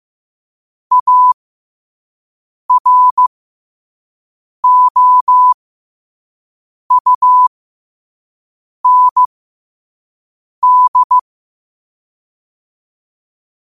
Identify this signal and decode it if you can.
Morse code, heard as a keyed tone, about 15 wpm, spaced as for 5 wpm, 1000 Hz. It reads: AROUND
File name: qotdfiles / 5wpm